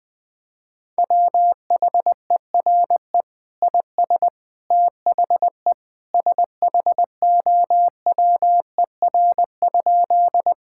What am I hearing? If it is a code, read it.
Morse code, heard as a keyed tone, about 20 wpm, 700 Hz. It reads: WHERE IS THE SHOWER?
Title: Sentences from Top 1600 Words 20wpm